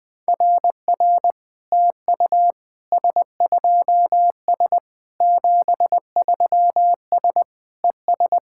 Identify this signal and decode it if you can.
Morse code, heard as a keyed tone, about 20 wpm, 700 Hz. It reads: RR TU S2S 73S ES
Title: SOTA QSO Practice 20wpm